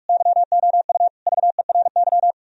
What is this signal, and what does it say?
Morse code, heard as a keyed tone, about 44 wpm, 700 Hz. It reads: YOU VERY